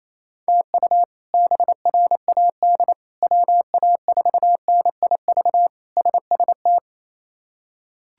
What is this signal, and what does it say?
Morse code, heard as a keyed tone, about 27 wpm, 700 Hz. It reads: TU BRAD WA4NIV SST
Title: K1USN SST QSO Practice - 28wpm